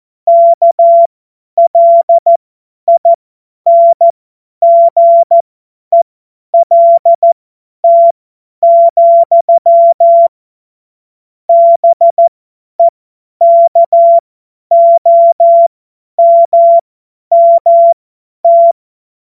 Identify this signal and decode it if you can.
Morse code, heard as a keyed tone, about 14 wpm, spaced as for 10 wpm, 680 Hz. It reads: KLINGELT, BEKOMMT
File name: Morsekurs / Witze / Witz 13 Februar 2024 50 F70